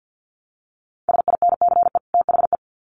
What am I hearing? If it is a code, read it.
Morse code, heard as a keyed tone, about 50 wpm, 700 Hz. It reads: SINCE THE
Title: Sets of 2 Words 50wpm